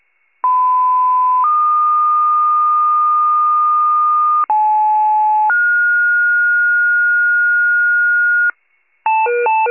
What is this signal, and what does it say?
Two-tone page, heard as a one-second tone, then a three-second tone, about 1010 then 1220 Hz, about 860 then 1430 Hz, with noise. It is heard from 0.4 to 8.5 s.